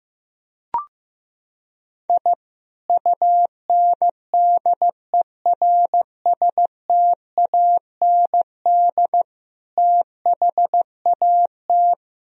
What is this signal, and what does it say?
Morse code, heard as a keyed tone, about 15 wpm, 700 Hz. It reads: I UNDERSTAND THAT
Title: Sentences from Top 1000 Words - Review Part 3 - 15wpm